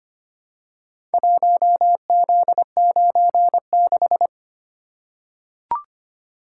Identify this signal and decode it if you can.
Morse code, heard as a keyed tone, about 25 wpm, 700 Hz. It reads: 1Z96
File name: practice / 25wpm / Four Letter-Number Combinations - 25wpm